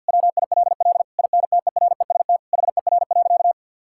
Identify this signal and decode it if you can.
Morse code, heard as a keyed tone, about 50 wpm, 700 Hz. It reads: WILL INTEREST HER.
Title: Sentences from Top 1000 Words - Review Part 5 - 50wpm